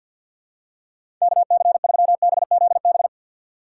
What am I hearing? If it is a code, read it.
Morse code, heard as a keyed tone, about 52 wpm, 700 Hz. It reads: K C 3 B Z B
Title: USA Call Signs - 6 Characters - 6x Character Spacing - 50wpm